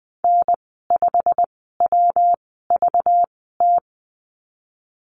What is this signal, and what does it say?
Morse code, heard as a keyed tone, about 20 wpm, spaced as for 15 wpm, 700 Hz. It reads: N5WVT
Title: 193 Lesson 31g - Letter v - Call Signs 20wpm